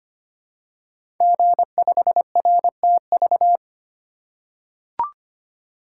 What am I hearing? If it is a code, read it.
Morse code, heard as a keyed tone, about 25 wpm, 700 Hz. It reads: G5RTV
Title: Code Groups - 5 Letter:Numbers - 25wpm